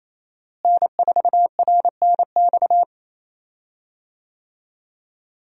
Morse code, heard as a keyed tone, about 28 wpm, 700 Hz. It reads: N4RNX
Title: USA Call Signs - 5 Characters - Repeated 2x - 28wpm